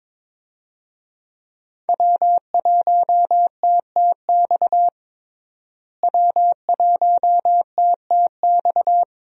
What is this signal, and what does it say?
Morse code, heard as a keyed tone, about 22 wpm, 700 Hz. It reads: W1TTX W1TTX
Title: USA Call Signs - 5 Characters - Repeated 2x - 22wpm